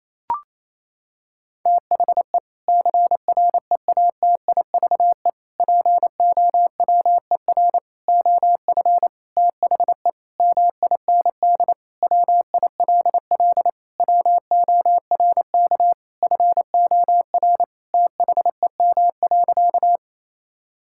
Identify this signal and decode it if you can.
Morse code, heard as a keyed tone, about 28 wpm, 700 Hz. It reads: THE CREATIVE POWER OF THE MIND WILL WORK FOR THEM.
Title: Sentences from Top 1000 Words - Review Part 5 - 28wpm